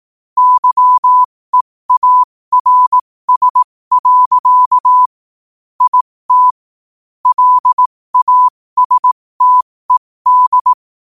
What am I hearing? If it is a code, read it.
Morse code, heard as a keyed tone, about 18 wpm, spaced as for 15 wpm, 1000 Hz. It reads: YEARS. IT LASTED
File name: qotdfiles / 15wpm